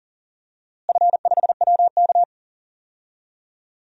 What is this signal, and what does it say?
Morse code, heard as a keyed tone, about 40 wpm, 700 Hz. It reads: F5WK
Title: DXCC-100 Call Signs - 4 Characters - Repeated 2x - 40wpm